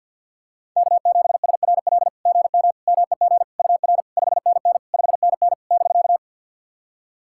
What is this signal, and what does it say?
Morse code, heard as a keyed tone, about 48 wpm, 690 Hz. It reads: K7SRL GM MEG UR 5NN 5NN <BK>
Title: SOTA QSO Practice 50wpm